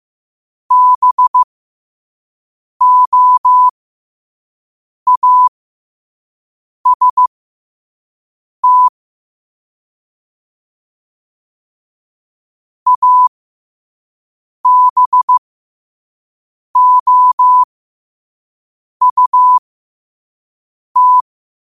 Morse code, heard as a keyed tone, about 15 wpm, spaced as for 5 wpm, 1000 Hz. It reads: BOAST ABOUT
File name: qotdfiles / 5wpm